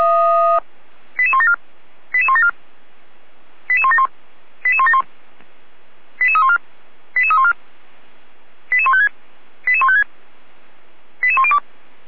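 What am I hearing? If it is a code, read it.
Five-tone selective call (ZVEI), heard as quick runs of five tones, tones of about 70 ms. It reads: 88174 88174 88171 88171 88314 88314 88156 88156 88102
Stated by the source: ZVEI five-tone selective call received by SDR.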